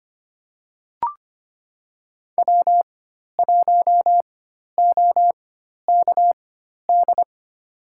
Morse code, heard as a keyed tone, about 25 wpm, spaced as for 12 wpm, 700 Hz. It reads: W1OKD
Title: USA Call Signs - 4x Character Spacing - 25wpm